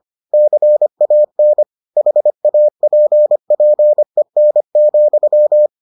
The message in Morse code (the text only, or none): CAN HAPPEN,